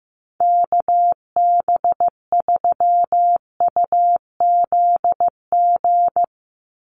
Morse code, heard as a keyed tone, about 15 wpm, 700 Hz. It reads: KB3UZG
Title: USA Call Signs 15wpm